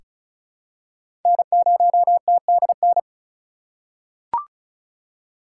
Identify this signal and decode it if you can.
Morse code, heard as a keyed tone, about 36 wpm, 700 Hz. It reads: N0TDN
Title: USA Call Signs - 5 Characters - 8x Character Spacing - 35wpm